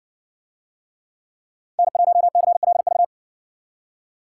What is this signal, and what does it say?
Morse code, heard as a keyed tone, about 60 wpm, 700 Hz. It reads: N1CLV